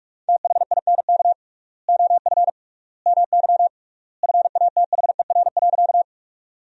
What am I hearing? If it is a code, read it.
Morse code, heard as a keyed tone, about 45 wpm, 700 Hz. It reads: THINK OF MY FATHER.
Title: 3x Word-Spacing - Sentences from Top 300 Words - 45wpm